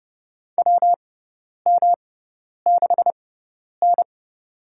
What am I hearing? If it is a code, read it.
Morse code, heard as a keyed tone, about 29 wpm, spaced as for 10 wpm, 700 Hz. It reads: WM6N
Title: USA Call Signs - 4 Characters - 6x Character Spacing - 30wpm